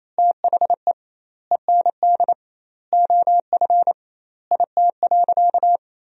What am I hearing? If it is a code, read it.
Morse code, heard as a keyed tone, about 28 wpm, 700 Hz. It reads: THE END OF IT.
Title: 2x Word-Spacing - Sentences from Top 200 Words - 28wpm